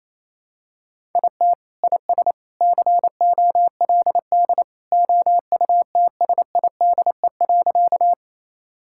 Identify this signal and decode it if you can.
Morse code, heard as a keyed tone, about 28 wpm, 700 Hz. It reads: IT IS COLD OUTSIDE.
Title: Sentences from Top 1000 Words - Review Part 6 - 28wpm